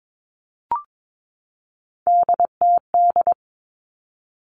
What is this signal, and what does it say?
Morse code, heard as a keyed tone, about 22 wpm, 700 Hz. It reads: DTD
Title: Three Challenging Letter-Number Combinations 22wpm